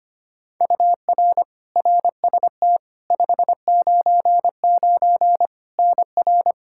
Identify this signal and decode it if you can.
Morse code, heard as a keyed tone, about 25 wpm, 700 Hz. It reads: UR RST 599 NR
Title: Practice QSO - Exchange 1 - Beginning - 25wpm